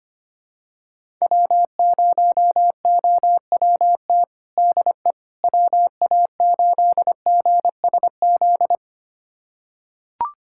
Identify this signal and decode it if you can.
Morse code, heard as a keyed tone, about 25 wpm, 700 Hz. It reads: W0OWT DE WA8GSZ
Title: USA Callsign DE Callsign 25wpm